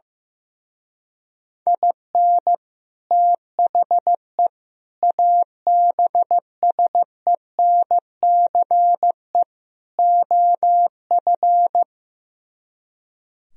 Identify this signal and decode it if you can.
Morse code, heard as a keyed tone, about 15 wpm, 700 Hz. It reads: IN THE ABSENCE OF
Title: Speed-Racing - Sets of 4 Words 15wpm